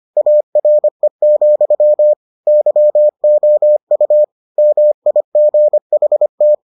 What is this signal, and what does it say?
Morse code, heard as a keyed tone, about 25 wpm, 600 Hz. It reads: ARE, YOU MIGHT